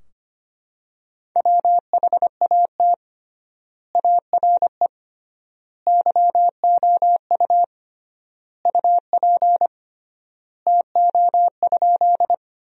Morse code, heard as a keyed tone, about 25 wpm, 700 Hz. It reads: WHAT ARE YOU UP TO?
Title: 3x Word-Spacing - Sentences from Top 100 Words - 25wpm